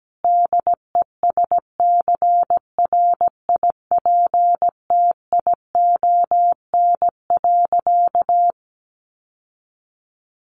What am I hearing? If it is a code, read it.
Morse code, heard as a keyed tone, about 17 wpm, 700 Hz. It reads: DESCRIPTION.
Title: Sentences from Top 1500 Words 17wpm